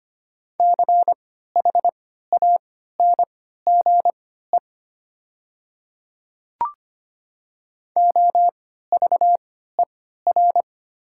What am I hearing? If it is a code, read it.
Morse code, heard as a keyed tone, about 25 wpm, spaced as for 14 wpm, 700 Hz. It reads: CHANGE OVER
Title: 3x Character Spacing - Top 300 Words - 25wpm